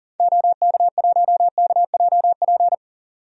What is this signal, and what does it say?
Morse code, heard as a keyed tone, about 40 wpm, 700 Hz. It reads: OK1KJP
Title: DXCC-100 Call Signs - 6 Characters - 40wpm